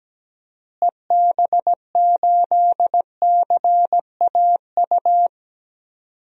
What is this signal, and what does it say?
Morse code, heard as a keyed tone, about 17 wpm, 700 Hz. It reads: EB8CAU
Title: DXCC-100 Call Signs - 6 Characters - Repeated 3x - 17wpm